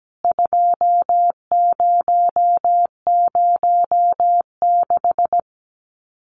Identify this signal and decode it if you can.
Morse code, heard as a keyed tone, about 17 wpm, 700 Hz. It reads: 2006
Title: CWT Exchange 17wpm